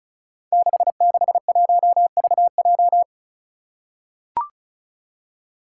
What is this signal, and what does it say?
Morse code, heard as a keyed tone, about 36 wpm, 700 Hz. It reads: 661VJ
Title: Five Challenging Letter-Number Combinations 35wpm